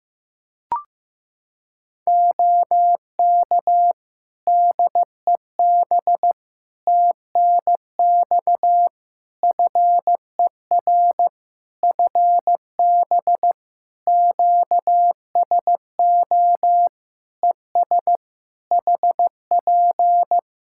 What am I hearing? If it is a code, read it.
Morse code, heard as a keyed tone, about 15 wpm, 700 Hz. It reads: OK DEB TNX FER FB QSO ES HP
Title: Practice QSO - Exchange 4 - Ending - 15wpm